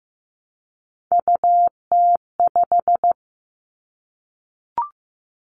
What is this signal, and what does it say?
Morse code, heard as a keyed tone, about 15 wpm, 700 Hz. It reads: UT5
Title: Callsign Prefixes for Top 25 DXCC - Repeated 3x - 15wpm